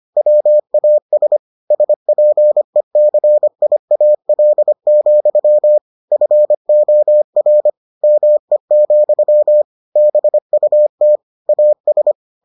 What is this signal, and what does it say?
Morse code, heard as a keyed tone, about 25 wpm, 600 Hz. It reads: WAS SPECIAL, FOR ME, BUT AS